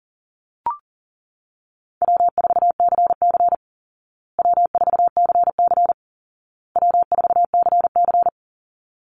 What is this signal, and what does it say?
Morse code, heard as a keyed tone, about 37 wpm, 700 Hz. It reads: W4CC W4CC W4CC
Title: Easy Call Signs Repeated 3x - 40wpm